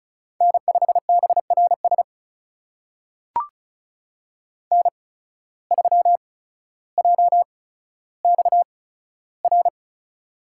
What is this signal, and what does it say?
Morse code, heard as a keyed tone, about 36 wpm, spaced as for 15 wpm, 700 Hz. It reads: N5BRS N3JXR
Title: USA Call Signs - 5 Characters - 8x Character Spacing - 35wpm